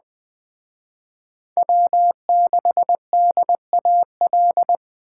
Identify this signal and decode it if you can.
Morse code, heard as a keyed tone, about 20 wpm, 700 Hz. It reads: W6DAL
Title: Speed Racing - USA 1x3 Call Signs 17wpm